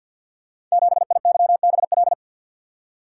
Morse code, heard as a keyed tone, about 47 wpm, 700 Hz. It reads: 7IYBL
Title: Code Groups - 5 Letter:Numbers - 50wpm